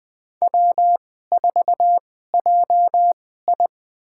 Morse code, heard as a keyed tone, about 21 wpm, spaced as for 15 wpm, 700 Hz. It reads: W4JI
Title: USA Call Signs - 2x Character Spacing - 20wpm